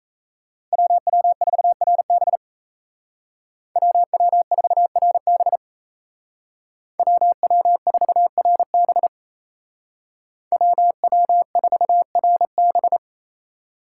Morse code, heard as a keyed tone, about 33 wpm, 700 Hz. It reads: WW4RB WW4RB WW4RB WW4RB
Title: Speed Racing - USA 2x2 Call Signs 28wpm